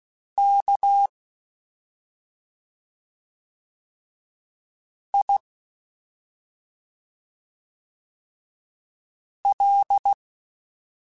Morse code, heard as a keyed tone, about 16 wpm, spaced as for 2 wpm, 800 Hz. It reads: KIL
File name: Special INGs-mp3s / Special INGs-mp3s / Slow ing @10